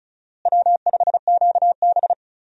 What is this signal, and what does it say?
Morse code, heard as a keyed tone, about 36 wpm, 700 Hz. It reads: W5QB
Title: USA Call Signs - 4 Characters - Repeated 3x - 35wpm